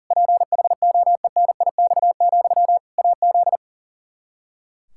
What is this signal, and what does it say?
Morse code, heard as a keyed tone, about 37 wpm, 700 Hz. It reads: PHOENIX, AZ
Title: Practice QSO - Exchange 1 - Beginning - 40wpm